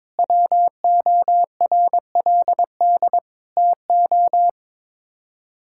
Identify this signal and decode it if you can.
Morse code, heard as a keyed tone, about 22 wpm, 700 Hz. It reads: WORLD TO
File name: practice / 22wpm / Sets of 7 Words - 22wpm